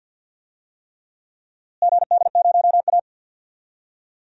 Morse code, heard as a keyed tone, about 45 wpm, 700 Hz. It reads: GD0A
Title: DXCC-100 Call Signs - 4 Characters - Repeated 3x - 50wpm